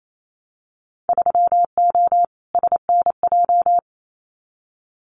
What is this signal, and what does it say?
Morse code, heard as a keyed tone, about 28 wpm, 700 Hz. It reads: 3O SNJ
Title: Winter Field Day Exchanges Encore 28wpm